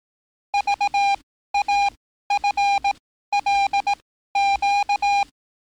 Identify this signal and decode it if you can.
Morse code, heard as a keyed tone, about 18 wpm, spaced as for 13 wpm, 810 Hz. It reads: VAFLQ
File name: training / NAVPERS 11049to11060 / NAVPERS 11054 RA RB